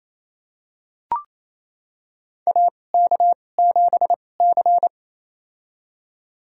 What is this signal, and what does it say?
Morse code, heard as a keyed tone, about 28 wpm, spaced as for 20 wpm, 700 Hz. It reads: AK7C
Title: USA Call Signs - 4 Characters - 2x Character Spacing - 28wpm